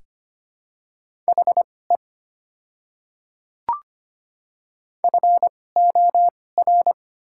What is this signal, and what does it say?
Morse code, heard as a keyed tone, about 25 wpm, spaced as for 18 wpm, 690 Hz. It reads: HE FOR
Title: 2x Character Spacing - Top 100 Words - 25wpm